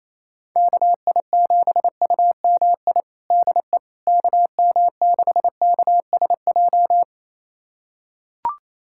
Morse code, heard as a keyed tone, about 28 wpm, 700 Hz. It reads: KI7UMI DE KM6KSJ
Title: USA Callsign DE Callsign - 6 Characters - 28wpm